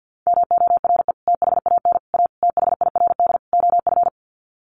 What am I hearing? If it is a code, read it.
Morse code, heard as a keyed tone, about 50 wpm, 700 Hz. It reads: MORE THAN A THIRD OF
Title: Sets of 5 Words Encore2 50wpm